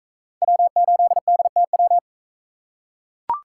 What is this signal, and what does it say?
Morse code, heard as a keyed tone, about 41 wpm, 700 Hz. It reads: W8DTW